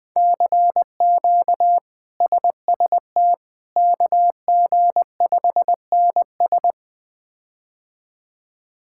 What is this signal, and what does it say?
Morse code, heard as a keyed tone, about 20 wpm, 700 Hz. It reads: CQ SST KG5NS
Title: K1USN SST QSO Practice - 20wpm